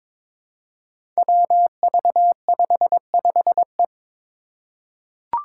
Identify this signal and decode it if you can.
Morse code, heard as a keyed tone, about 22 wpm, 700 Hz. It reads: WV55E
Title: Five Challenging Letter-Number Combinations 22wpm